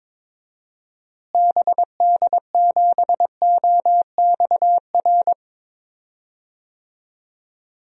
Morse code, heard as a keyed tone, about 21 wpm, 700 Hz. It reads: BD7OXR
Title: DXCC-100 Call Signs - 6 Characters - Repeated 2x - 22wpm